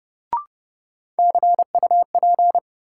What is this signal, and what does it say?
Morse code, heard as a keyed tone, about 30 wpm, 700 Hz. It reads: CUP
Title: Sentences from Top 1600 Words - New Word Drill Alt - 30wpm